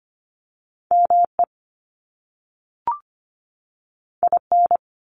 Morse code, heard as a keyed tone, about 25 wpm, 700 Hz. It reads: ME IN
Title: US States 25wpm